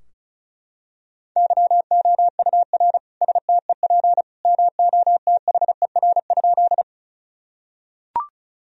Morse code, heard as a keyed tone, about 36 wpm, 700 Hz. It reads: YOUR STEP MOTHER?
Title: Sentences from Top 1000 Words - Review Part 2 - 35wpm